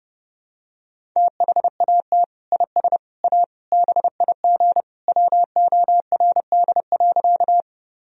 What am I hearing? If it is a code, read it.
Morse code, heard as a keyed tone, about 30 wpm, 700 Hz. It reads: THAT IS A BIG WORD.
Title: Sentences from Top 1000 Words - Review Part 1 - 30wpm